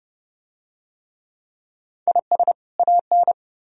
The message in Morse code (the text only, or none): IS AN